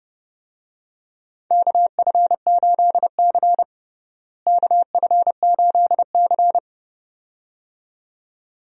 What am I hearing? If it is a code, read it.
Morse code, heard as a keyed tone, about 32 wpm, 700 Hz. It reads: KF8C KF8C
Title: USA Call Signs - 4 Characters - Repeated 2x - 30wpm